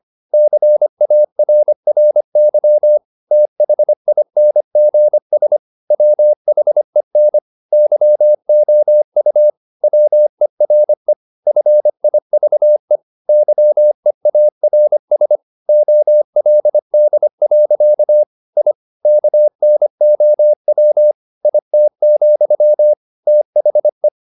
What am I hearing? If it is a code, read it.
Morse code, heard as a keyed tone, about 25 wpm, 600 Hz. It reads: CARRY THINGS WHEN YOU WERE FIVE YEARS OLD. I KNOW IT, THE
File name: the old man and the sea part 1 25wpm